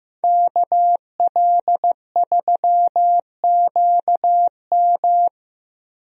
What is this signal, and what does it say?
Morse code, heard as a keyed tone, about 15 wpm, 700 Hz. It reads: KL3QM
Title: USA Call Signs - 5 Characters - Repeated 2x - 15wpm